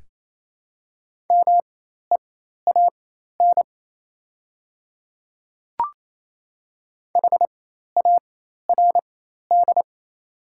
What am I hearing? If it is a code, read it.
Morse code, heard as a keyed tone, about 28 wpm, spaced as for 13 wpm, 690 Hz. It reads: MEAN HARD
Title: 4x Character Spacing - Top 300 Words - 28wpm